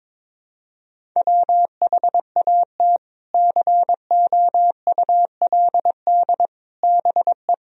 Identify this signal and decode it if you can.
Morse code, heard as a keyed tone, about 22 wpm, 700 Hz. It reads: WHAT COULD BE
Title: Sentences from Top 1000 Words - Review Part 1 - 22wpm